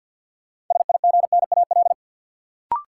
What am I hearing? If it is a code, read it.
Morse code, heard as a keyed tone, about 47 wpm, 700 Hz. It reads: SIGNAL
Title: Sentences from Top 1400 Words - New Word Drill - 50wpm